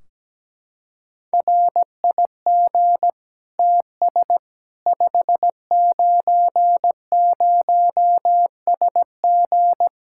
Morse code, heard as a keyed tone, about 17 wpm, 700 Hz. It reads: RIG TS 590SG
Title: K1USN SST QSO Practice 17wpm